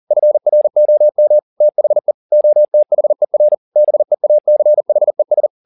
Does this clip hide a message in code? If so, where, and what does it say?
Yes: Morse code from 0.1 to 5.5 s, heard as a keyed tone, about 40 wpm, 600 Hz. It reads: FROM THE OTHER BEACHES